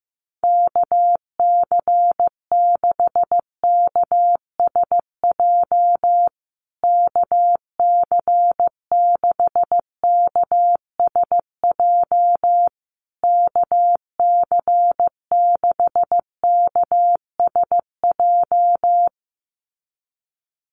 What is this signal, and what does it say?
Morse code, heard as a keyed tone, about 15 wpm, 700 Hz. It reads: KC6KSJ KC6KSJ KC6KSJ
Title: USA Call Signs Repeated 3x 15wpm